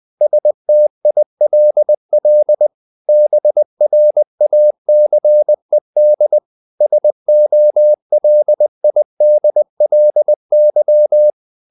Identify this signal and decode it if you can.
Morse code, heard as a keyed tone, about 20 wpm, 600 Hz. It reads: STILL BRACED SOLIDLY
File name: the old man and the sea / the old man and the sea part 1 20wpm